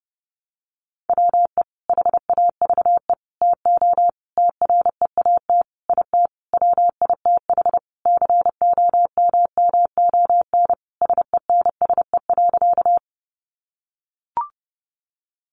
Morse code, heard as a keyed tone, about 30 wpm, 700 Hz. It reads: WE HAVE TO TREAT IT WITH COMMON SENSE.
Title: Sentences from Top 1300 Words 30wpm